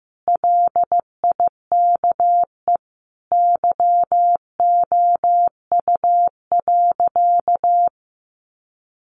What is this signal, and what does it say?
Morse code, heard as a keyed tone, about 15 wpm, 700 Hz. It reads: LIKE YOU.
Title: Sentences from Top 1100 Words 15wpm